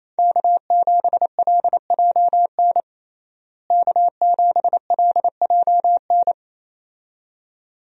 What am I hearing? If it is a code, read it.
Morse code, heard as a keyed tone, about 27 wpm, 700 Hz. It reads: K7LJN K7LJN
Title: USA Call Signs - 5 Characters - Repeated 2x - 28wpm